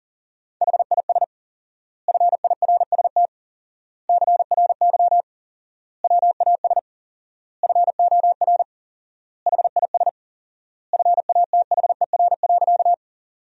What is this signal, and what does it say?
Morse code, heard as a keyed tone, about 37 wpm, 700 Hz. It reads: HIS FIRST CRY WAS FOR HIS FATHER.
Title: 4x Word-Spacing - Sentences from Top 500 Words - 40wpm